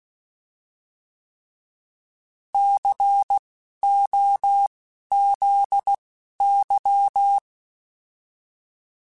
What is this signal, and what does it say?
Morse code, heard as a keyed tone, about 16 wpm, spaced as for 11 wpm, 800 Hz. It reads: COZY
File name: Alphabetical / Alphabetical mp3s / Cs @10